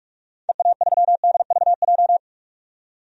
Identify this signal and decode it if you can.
Morse code, heard as a keyed tone, about 47 wpm, 700 Hz. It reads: EA3DVJ
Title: DXCC-100 Call Signs - 6 Characters - 45wpm